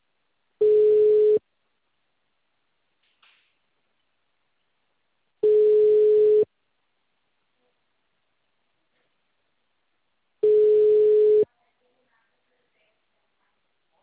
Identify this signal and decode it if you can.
Ringback tone (Europe, 425 Hz).
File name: out- -2017- - - .31627